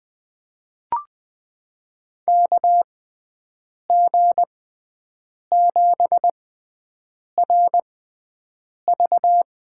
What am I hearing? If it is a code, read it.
Morse code, heard as a keyed tone, about 20 wpm, spaced as for 7 wpm, 700 Hz. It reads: KG7RV
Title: USA Call Signs - 6 Characters - 6x Character Spacing - 20wpm